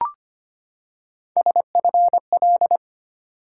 Morse code, heard as a keyed tone, about 25 wpm, 700 Hz. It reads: SFL
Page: morse-code-ninja-practice-25wpm